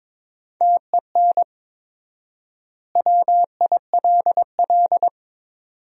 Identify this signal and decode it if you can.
Morse code, heard as a keyed tone, about 22 wpm, 700 Hz. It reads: TEN WILL
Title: 4x Word-Spacing - Sentences from Top 400 Words - 22wpm